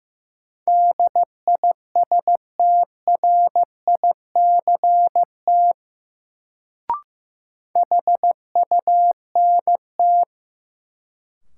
Morse code, heard as a keyed tone, about 15 wpm, 700 Hz. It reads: DISTRICT HUNT
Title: Sentences from Top 1600 Words - New Word Drill Alt - 15wpm